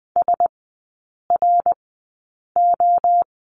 Morse code, heard as a keyed tone, about 20 wpm, spaced as for 8 wpm, 700 Hz. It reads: SRO
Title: 087 Lesson 15e - Number 5 - Three Characters 20wpm